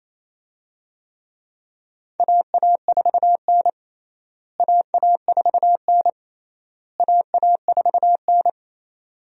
Morse code, heard as a keyed tone, about 28 wpm, 700 Hz. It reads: AA4N AA4N AA4N
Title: USA Call Signs - 4 Characters - Repeated 3x - 28wpm